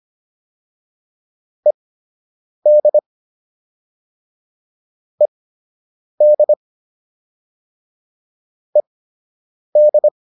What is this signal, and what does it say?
Morse code, heard as a keyed tone, about 25 wpm, spaced as for 8 wpm, 600 Hz. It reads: ED ED ED